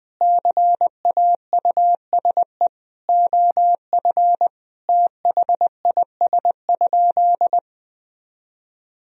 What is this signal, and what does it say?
Morse code, heard as a keyed tone, about 20 wpm, 700 Hz. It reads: CAUSE OF THIS?
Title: Sentences from Top 1000 Words - Review Part 1 - 20wpm